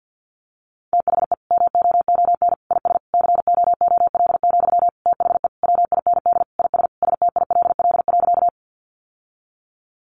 Morse code, heard as a keyed tone, about 50 wpm, 700 Hz. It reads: THE MOON IS COOL, THE WIND IS STILL.
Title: Sentences from Top 1200 Words 50wpm